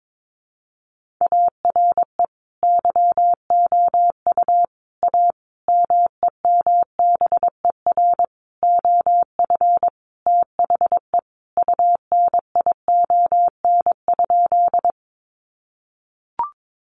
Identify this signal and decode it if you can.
Morse code, heard as a keyed tone, about 22 wpm, 700 Hz. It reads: ARE YOU A MEMBER OF THE UNION?
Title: Sentences from Top 1400 Words 22wpm